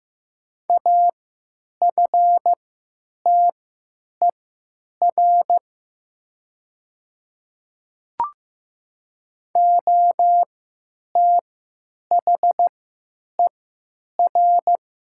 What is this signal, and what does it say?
Morse code, heard as a keyed tone, about 15 wpm, spaced as for 8 wpm, 700 Hz. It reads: AFTER OTHER
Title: 3x Character Spacing - Top 200 Words - 15wpm